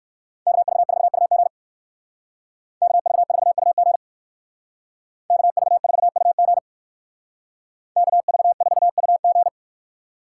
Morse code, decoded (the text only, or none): KV4UG KV4UG KV4UG KV4UG